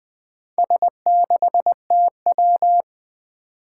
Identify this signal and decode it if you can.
Morse code, heard as a keyed tone, about 20 wpm, 700 Hz. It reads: S6TW
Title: Four Challenging Letter-Number Combinations 20wpm